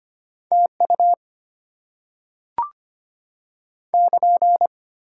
Morse code, heard as a keyed tone, about 25 wpm, 700 Hz. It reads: TU <KN>
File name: Top 90 QSO Elements - 8x Character Spacing - 25wpm